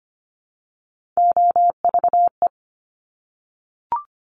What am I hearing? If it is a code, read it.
Morse code, heard as a keyed tone, about 25 wpm, 700 Hz. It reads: OVE
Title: Top Three Letter Combinations in Words 25wpm